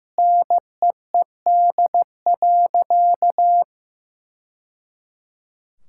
Morse code, heard as a keyed tone, about 15 wpm, 700 Hz. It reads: NEED.
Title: Sentences from Top 1000 Words - Review Part 5 - 15wpm